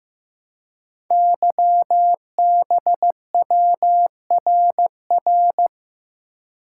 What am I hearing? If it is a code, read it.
Morse code, heard as a keyed tone, about 15 wpm, 700 Hz. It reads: YBWRR
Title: Code Groups - 5 Letters - 15wpm